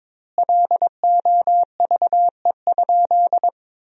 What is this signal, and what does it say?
Morse code, heard as a keyed tone, about 23 wpm, 700 Hz. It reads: LOVE?
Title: 2x Word-Spacing - Sentences from Top 400 Words - 22wpm